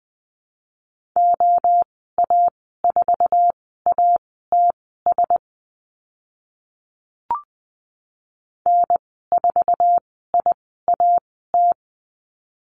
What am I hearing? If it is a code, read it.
Morse code, heard as a keyed tone, about 20 wpm, spaced as for 14 wpm, 700 Hz. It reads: OA4ATS N4IAT
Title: 076 Lesson 13g - Letter L - Call Signs 20wpm